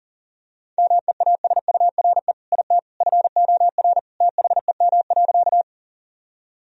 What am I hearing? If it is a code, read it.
Morse code, heard as a keyed tone, about 37 wpm, 700 Hz. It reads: MEASURE IT FOR THEM.
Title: Sentences from Top 1000 Words - Review Part 2 - 40wpm